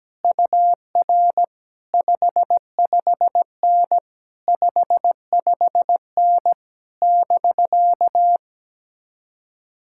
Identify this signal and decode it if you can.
Morse code, heard as a keyed tone, about 17 wpm, 700 Hz. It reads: UR 55N 55N <BK>
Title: Speed-Racing Common QSO Phrases - 17wpm